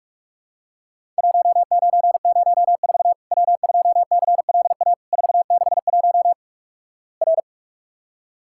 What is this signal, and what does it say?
Morse code, heard as a keyed tone, about 46 wpm, 700 Hz. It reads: 1904 W2CLA 461 R
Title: Worked All Europe DX Contest - QTC Exchange - 45wpm